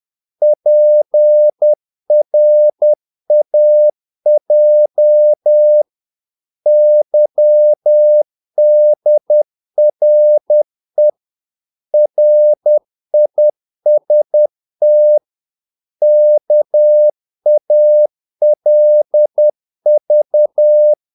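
Morse code, heard as a keyed tone, about 10 wpm, 600 Hz. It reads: PRAJ YDRE RIST KALV
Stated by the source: Morse practice of short Danish words at 10 wpm, from CW med Gnister.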